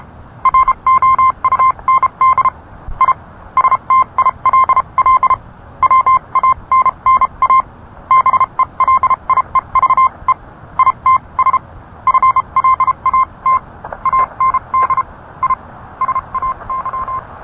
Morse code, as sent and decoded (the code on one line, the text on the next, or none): .-. --- ..- -. -.. / .. / ... - .. .-.. .-.. / .-- .- -. -. .- / -... .-.. .. ...- . / .. - ... / ..-. .-.. .- - / .- -. -.. / .. / ... .- -.--
ROUND I STILL WANNA BELIEVE ITS FLAT AND I SAY